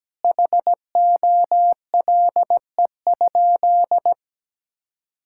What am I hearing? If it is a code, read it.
Morse code, heard as a keyed tone, about 17 wpm, 700 Hz. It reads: HOLE?
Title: Sentences from Top 1600 Words 17wpm